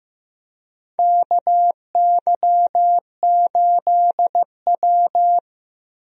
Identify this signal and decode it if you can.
Morse code, heard as a keyed tone, about 15 wpm, 700 Hz. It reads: KY8W
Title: USA Call Signs - 4 Characters - Repeated 3x - 15wpm